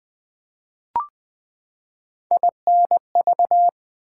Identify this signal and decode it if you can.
Morse code, heard as a keyed tone, about 20 wpm, 700 Hz. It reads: INV
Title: Build A Word - 3 Letters At A Time - 9-Letter Words - 20wpm